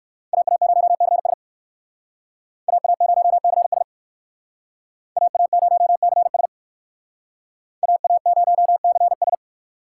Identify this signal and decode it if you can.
Morse code, heard as a keyed tone, about 58 wpm, 700 Hz. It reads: AA0CS AA0CS AA0CS AA0CS